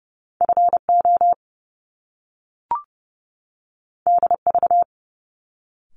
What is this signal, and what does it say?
Morse code, heard as a keyed tone, about 30 wpm, 700 Hz. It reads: FO DV
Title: Top Two Letter Combinations in Words 30wpm